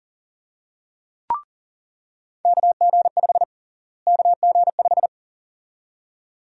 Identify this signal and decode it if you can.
Morse code, heard as a keyed tone, about 40 wpm, 700 Hz. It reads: KG5 KG5
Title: USA Call Sign Prefixes - Repeated 2x - 40wpm